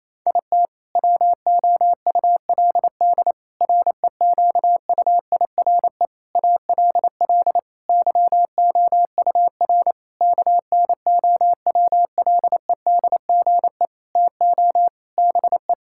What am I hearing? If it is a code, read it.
Morse code, heard as a keyed tone, about 28 wpm, 700 Hz. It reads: IT WOULD REQUIRE ALL YOUR KNOWLEDGE TO BE
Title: Sentences from Top 1000 Words - Review Part 6 - 28wpm